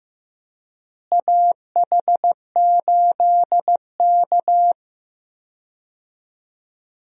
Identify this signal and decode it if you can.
Morse code, heard as a keyed tone, about 15 wpm, 700 Hz. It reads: AH8K
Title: USA Call Signs - 4 Characters - Repeated 3x - 15wpm